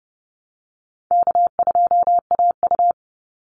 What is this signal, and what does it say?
Morse code, heard as a keyed tone, about 30 wpm, 700 Hz. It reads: K2AU
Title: Call Signs - 30wpm